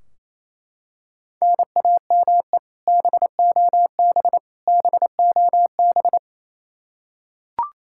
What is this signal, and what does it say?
Morse code, heard as a keyed tone, about 27 wpm, 700 Hz. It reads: NAME BOB BOB
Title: K1USN SST QSO Practice 28wpm